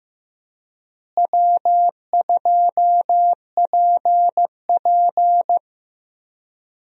Morse code, heard as a keyed tone, about 15 wpm, 700 Hz. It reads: W2PP
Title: USA Call Signs - 4 Characters - Repeated 3x - 15wpm